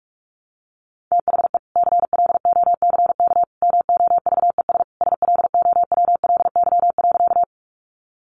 Morse code, heard as a keyed tone, about 44 wpm, 700 Hz. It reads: THE CLOCK MOVES SLOWLY.
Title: Easy Phrases 45wpm